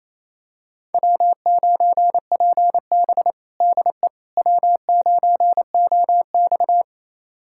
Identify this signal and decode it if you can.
Morse code, heard as a keyed tone, about 28 wpm, 700 Hz. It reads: W9PB DE W9OX
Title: USA Callsign DE Callsign - 4 Characters - 28wpm